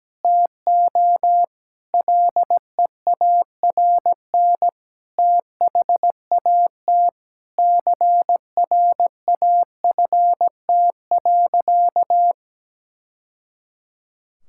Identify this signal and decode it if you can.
Morse code, heard as a keyed tone, about 17 wpm, 700 Hz. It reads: TO LEARN THAT CRAFT.
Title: Sentences from Top 1000 Words - Review Part 6 - 17wpm